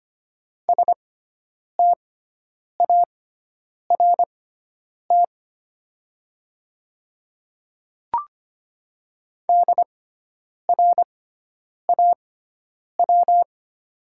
Morse code, heard as a keyed tone, about 25 wpm, spaced as for 9 wpm, 700 Hz. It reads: START DRAW
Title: 6x Character Spacing - Top 300 Words - 25wpm